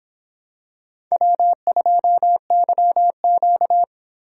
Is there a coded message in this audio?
Yes: Morse code from 1.1 to 3.8 s, heard as a keyed tone, about 26 wpm, 700 Hz. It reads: W2YQ